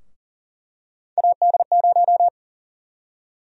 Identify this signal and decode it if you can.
Morse code, heard as a keyed tone, about 38 wpm, 700 Hz. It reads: AD0
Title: USA Call Sign Prefixes - 40wpm